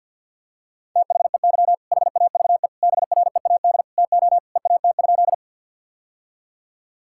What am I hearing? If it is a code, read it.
Morse code, heard as a keyed tone, about 50 wpm, 700 Hz. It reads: THEY HAVE BREAD TO EAT?
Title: Sentences from Top 1600 Words 50wpm